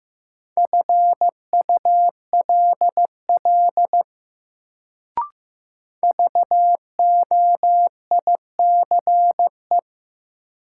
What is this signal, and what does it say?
Morse code, heard as a keyed tone, about 15 wpm, 700 Hz. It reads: FULL VOICE
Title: Sentences from Top 500 Words - New Word Drill - 15wpm